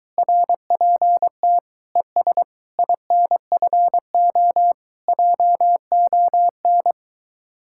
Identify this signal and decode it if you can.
Morse code, heard as a keyed tone, about 24 wpm, 700 Hz. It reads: RPT ES INFO JON